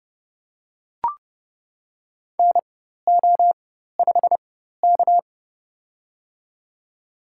Morse code, heard as a keyed tone, about 29 wpm, spaced as for 14 wpm, 700 Hz. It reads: NO5K
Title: USA Call Signs - 4 Characters - 4x Character Spacing - 30wpm